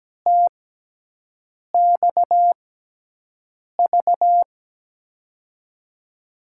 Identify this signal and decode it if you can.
Morse code, heard as a keyed tone, about 17 wpm, spaced as for 6 wpm, 700 Hz. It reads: TXV
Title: USA Call Signs - 6 Characters - 6x Character Spacing - 17wpm